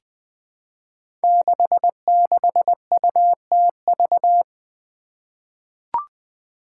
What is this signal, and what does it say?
Morse code, heard as a keyed tone, about 20 wpm, 700 Hz. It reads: 66UTV
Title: Five Challenging Letter-Number Combinations 20wpm